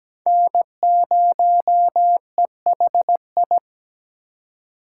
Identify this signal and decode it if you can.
Morse code, heard as a keyed tone, about 20 wpm, 700 Hz. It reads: N0EHI
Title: Speed Racing - USA 1x3 Call Signs 17wpm